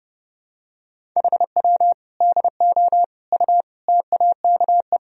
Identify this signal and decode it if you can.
Morse code, heard as a keyed tone, about 30 wpm, 700 Hz. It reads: HW DO U TAKE
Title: Sentences from Top 200 Words and Abbreviations 30wpm